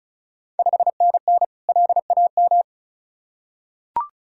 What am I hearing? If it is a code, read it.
Morse code, heard as a keyed tone, about 36 wpm, 690 Hz. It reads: 5NN LAM
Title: Ontario QSO Party Exchange 35wpm